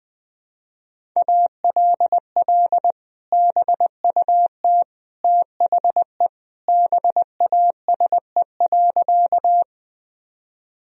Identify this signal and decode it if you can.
Morse code, heard as a keyed tone, about 20 wpm, 700 Hz. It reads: ALL BUT THE BASE.
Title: Sentences from Top 1000 Words - Review Part 1 - 20wpm